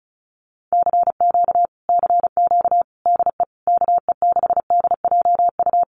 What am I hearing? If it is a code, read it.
Morse code, heard as a keyed tone, about 35 wpm, 700 Hz. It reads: CQ CQ DE KE6DJU
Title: Common QSO Phrases Encore 35wpm